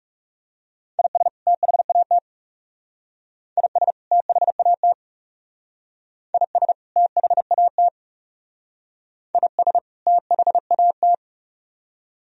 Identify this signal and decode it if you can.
Morse code, heard as a keyed tone, about 36 wpm, 700 Hz. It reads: IS THAT IS THAT IS THAT IS THAT
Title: Speed-Racing - Sets of 2 Words 30wpm